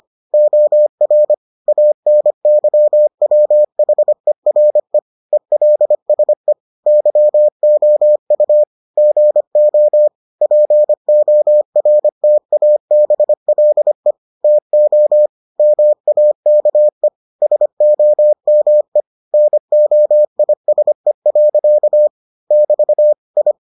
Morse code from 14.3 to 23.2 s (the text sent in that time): TO MAKE SOME NOISE. <BT>